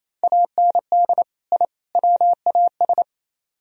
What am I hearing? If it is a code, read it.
Morse code, heard as a keyed tone, about 35 wpm, 700 Hz. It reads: AND I WAS